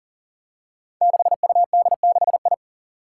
Morse code, heard as a keyed tone, about 40 wpm, 700 Hz. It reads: 6UD6I
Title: Five Challenging Letter-Number Combinations 40wpm